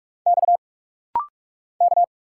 Morse code, heard as a keyed tone, about 45 wpm, 690 Hz. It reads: X K
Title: Single Letters - Rapid-Fire 45wpm